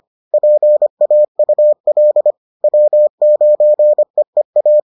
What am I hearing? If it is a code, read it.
Morse code, heard as a keyed tone, about 25 wpm, 600 Hz. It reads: PAUL W9EEA